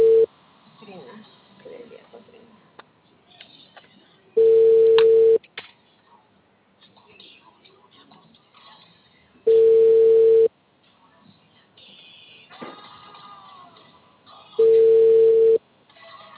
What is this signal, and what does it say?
Ringback tone (China, 450 Hz), with speech.